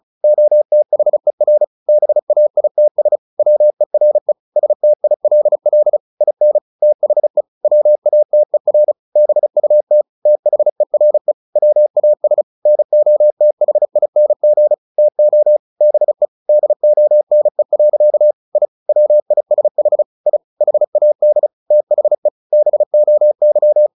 Morse code, heard as a keyed tone, about 34 wpm, 600 Hz. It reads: OTHER BAITS WERE STILL IN THE WATER BUT THERE WAS NOTHING TO BE DONE. I WISH I HAD THE BOY
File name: the old man and the sea / the old man and the sea 1 35wpm